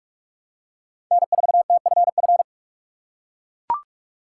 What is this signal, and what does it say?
Morse code, heard as a keyed tone, about 44 wpm, 700 Hz. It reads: N4TFF
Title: USA Call Signs - 3x Character Spacing - 45wpm